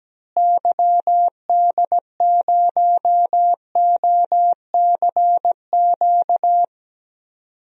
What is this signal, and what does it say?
Morse code, heard as a keyed tone, about 17 wpm, 700 Hz. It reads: YD0OCQ
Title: DXCC-100 Call Signs - 6 Characters - Repeated 3x - 17wpm